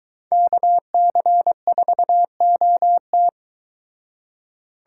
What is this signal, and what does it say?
Morse code, heard as a keyed tone, about 23 wpm, 700 Hz. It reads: KC4OT